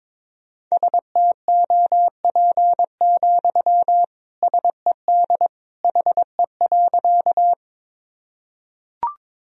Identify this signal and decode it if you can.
Morse code, heard as a keyed tone, about 22 wpm, 700 Hz. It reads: STOP, SED HE.
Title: Sentences from Top 300 Words and Abbreviations 22wpm